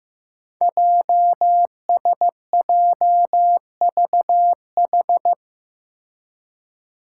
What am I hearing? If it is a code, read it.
Morse code, heard as a keyed tone, about 15 wpm, 700 Hz. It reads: JSJVH
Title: Five Challenging Letter-Number Combinations 15wpm